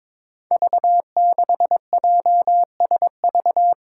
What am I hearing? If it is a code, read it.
Morse code, heard as a keyed tone, about 22 wpm, 700 Hz. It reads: V6JSV
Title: Five Challenging Letter-Number Combinations 22wpm